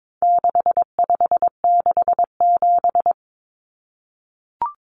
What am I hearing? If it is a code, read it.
Morse code, heard as a keyed tone, about 22 wpm, 700 Hz. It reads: 6567
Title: Numbers - 4-Digit 22wpm